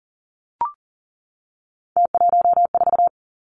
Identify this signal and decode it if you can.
Morse code, heard as a keyed tone, about 40 wpm, 700 Hz. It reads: T14
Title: Three Challenging Letter-Number Combinations 40wpm